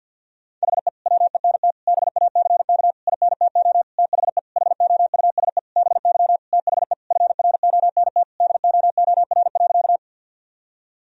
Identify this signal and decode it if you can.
Morse code, heard as a keyed tone, about 50 wpm, 700 Hz. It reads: HE WENT BACK INTO THE HOUSE BY THE FRONT DOOR.
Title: Sentences from Top 1000 Words - Review Part 2 - 50wpm